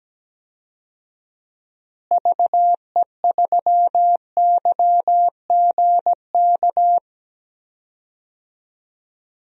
Morse code, heard as a keyed tone, about 17 wpm, 700 Hz. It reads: VE3YGK
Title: DXCC-100 Call Signs - 6 Characters - Repeated 3x - 17wpm